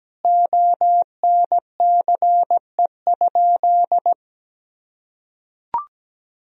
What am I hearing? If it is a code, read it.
Morse code, heard as a keyed tone, about 17 wpm, 700 Hz. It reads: ONCE?
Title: Sentences from Top 1000 Words - Review Part 1 - 17wpm